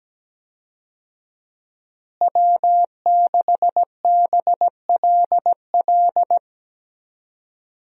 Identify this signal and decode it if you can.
Morse code, heard as a keyed tone, about 17 wpm, 700 Hz. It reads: W6BLL
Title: USA Call Signs - 5 Characters - Repeated 2x - 17wpm